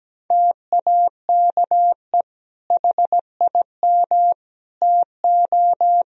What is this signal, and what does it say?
Morse code, heard as a keyed tone, about 17 wpm, 700 Hz. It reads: TAKE HIM TO
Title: Sentences from Top 1000 Words - Review Part 4 - 17wpm